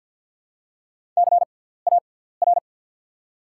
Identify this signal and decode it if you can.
Morse code, heard as a keyed tone, about 47 wpm, spaced as for 17 wpm, 690 Hz. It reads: CAR
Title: 6x Character Spacing - Top 500 Words - 50wpm